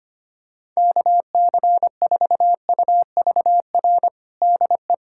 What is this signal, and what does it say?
Morse code, heard as a keyed tone, about 25 wpm, 700 Hz. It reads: KC4UVR DE
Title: USA Callsign DE Callsign 25wpm